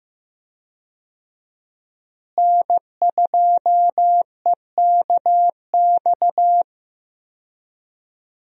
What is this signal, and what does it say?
Morse code, heard as a keyed tone, about 15 wpm, 700 Hz. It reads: N2EKX
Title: USA Call Signs - 5 Characters - Repeated 2x - 15wpm